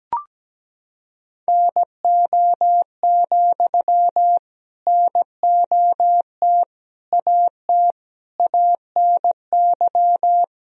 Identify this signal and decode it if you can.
Morse code, heard as a keyed tone, about 17 wpm, 700 Hz. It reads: NO, NOT AT ANY
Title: Sentences from Top 1000 Words - Review Part 4 - 17wpm